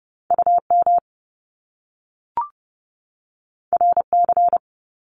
Morse code, heard as a keyed tone, about 30 wpm, 700 Hz. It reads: U M R C
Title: Top Two Letter Combinations in Words 30wpm